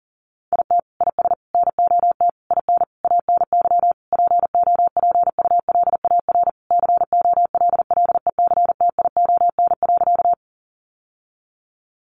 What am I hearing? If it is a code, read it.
Morse code, heard as a keyed tone, about 40 wpm, 700 Hz. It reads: IT IS NOT IN ANY POPULAR COLLECTION.
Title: Sentences from Top 1200 Words 40wpm